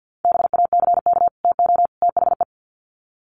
Morse code, heard as a keyed tone, about 50 wpm, 700 Hz. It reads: BACK TO THE
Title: Sets of 3 Words 50wpm